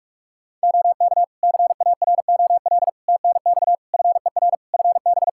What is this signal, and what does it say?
Morse code, heard as a keyed tone, about 44 wpm, 700 Hz. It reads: OK CAROL TNX FER FB
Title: K1USN SST QSO Practice 45wpm